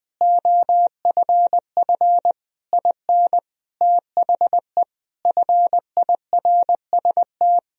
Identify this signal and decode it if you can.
Morse code, heard as a keyed tone, about 20 wpm, 700 Hz. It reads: OFF IN THE FIRST
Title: Sentences from Top 200 Words and Abbreviations 20wpm